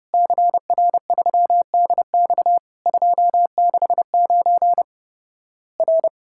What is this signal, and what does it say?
Morse code, heard as a keyed tone, about 30 wpm, 700 Hz. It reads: CR3DX 269 R
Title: Worked All Europe DX Contest - QTC Exchange - 30wpm